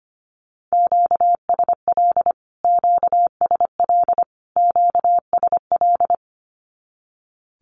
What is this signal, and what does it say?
Morse code, heard as a keyed tone, about 25 wpm, 700 Hz. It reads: QSL QSL QSL
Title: Q-Codes Repeated 3x 25wpm